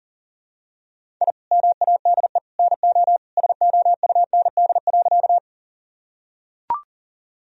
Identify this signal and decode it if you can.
Morse code, heard as a keyed tone, about 40 wpm, 700 Hz. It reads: I MADE NO SOUND.
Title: Sentences from Top 1000 Words - Review Part 1 - 40wpm